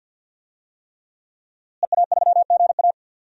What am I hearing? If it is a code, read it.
Morse code, heard as a keyed tone, about 50 wpm, 700 Hz. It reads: EA3GA
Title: DXCC-100 Call Signs - 5 Characters - Repeated 3x - 50wpm